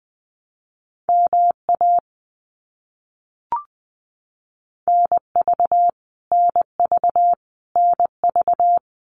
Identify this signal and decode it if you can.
Morse code, heard as a keyed tone, about 20 wpm, 700 Hz. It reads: MA NV NV NV
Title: US States Repeated 3x 20wpm